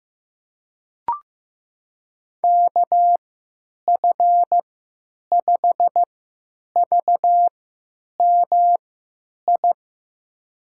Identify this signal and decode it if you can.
Morse code, heard as a keyed tone, about 15 wpm, spaced as for 9 wpm, 700 Hz. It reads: KF5VMI
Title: USA Call Signs - 6 Characters - 3x Character Spacing - 15wpm